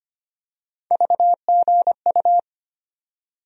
Morse code, heard as a keyed tone, about 25 wpm, 700 Hz. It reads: VGU
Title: USA Call Sign Postfixes - 3 Letters 25wpm